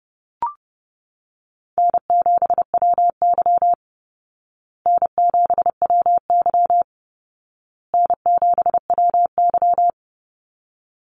Morse code, heard as a keyed tone, about 30 wpm, 700 Hz. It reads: N7WY N7WY N7WY
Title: Call Signs Repeated 2x - 30wpm